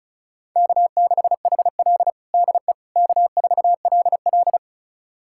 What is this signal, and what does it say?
Morse code, heard as a keyed tone, about 34 wpm, 700 Hz. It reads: K6HL DE K4LL
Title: USA Callsign DE Callsign - 4 Characters - 35wpm